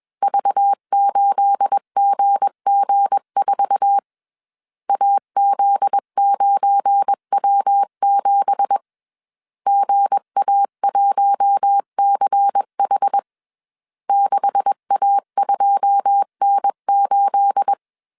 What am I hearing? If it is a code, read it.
Morse code, heard as a keyed tone, about 21 wpm, 800 Hz. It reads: V8GG4 AZ9W7 GA1CH 6A2N8